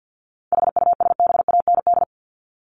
Morse code, heard as a keyed tone, about 50 wpm, 700 Hz. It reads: HUSBAND
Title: Top 1501-1750 Words 50wpm